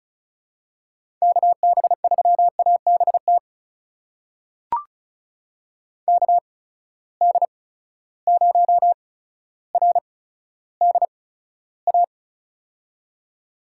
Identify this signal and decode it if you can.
Morse code, heard as a keyed tone, about 36 wpm, spaced as for 15 wpm, 700 Hz. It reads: KB3ABT KD0RDA
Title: USA Call Signs - 6 Characters - 8x Character Spacing - 35wpm